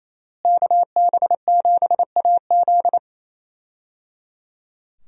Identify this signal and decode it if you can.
Morse code, heard as a keyed tone, about 28 wpm, 700 Hz. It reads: KB7AZ
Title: USA Call Signs - 5 Characters - Repeated 3x - 28wpm